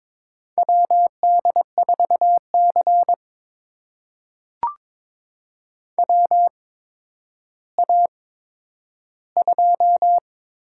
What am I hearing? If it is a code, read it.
Morse code, heard as a keyed tone, about 22 wpm, spaced as for 11 wpm, 700 Hz. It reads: WD4C WA2
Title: USA Call Signs - 8x Character Spacing - 22wpm